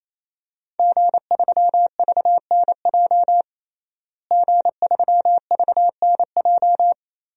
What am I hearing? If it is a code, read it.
Morse code, heard as a keyed tone, about 27 wpm, 700 Hz. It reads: G3VNJ G3VNJ
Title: DXCC-100 Call Signs - 5 Characters - Repeated 3x - 28wpm